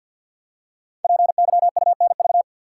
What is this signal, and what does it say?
Morse code, heard as a keyed tone, about 50 wpm, 700 Hz. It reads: PYUNV
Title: Code Groups - 5 Letters - 50wpm